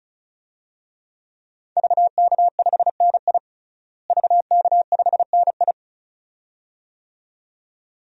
Morse code, heard as a keyed tone, about 36 wpm, 700 Hz. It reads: VK5NI VK5NI
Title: DXCC-100 Call Signs - 5 Characters - Repeated 2x - 35wpm